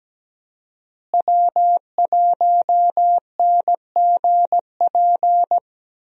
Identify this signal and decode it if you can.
Morse code, heard as a keyed tone, about 17 wpm, 700 Hz. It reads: W1NGP
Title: USA Call Signs - 5 Characters - Repeated 3x - 17wpm